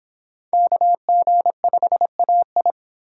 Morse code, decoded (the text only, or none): KG5AI